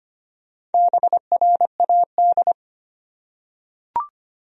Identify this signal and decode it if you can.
Morse code, heard as a keyed tone, about 25 wpm, 700 Hz. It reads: BRAD
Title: USA Common Names - Max 4 Letters - 25wpm